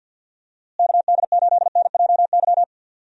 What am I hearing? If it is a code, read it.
Morse code, heard as a keyed tone, about 50 wpm, 700 Hz. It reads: KD8NJY
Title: USA Call Signs - 6 Characters - Repeated 2x - 50wpm